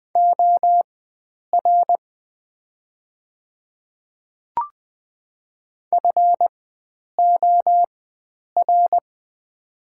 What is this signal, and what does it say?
Morse code, heard as a keyed tone, about 20 wpm, spaced as for 9 wpm, 690 Hz. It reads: OR FOR
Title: 4x Character Spacing - Top 100 Words - 20wpm